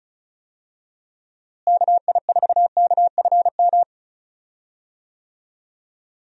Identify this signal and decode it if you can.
Morse code, heard as a keyed tone, about 36 wpm, 700 Hz. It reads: KI4KFM
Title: USA Call Signs - 6 Characters - Repeated 3x - 35wpm